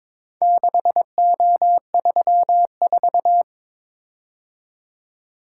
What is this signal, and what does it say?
Morse code, heard as a keyed tone, about 22 wpm, 700 Hz. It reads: 6O34
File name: practice / 22wpm / Four Letter-Number Combinations - 22wpm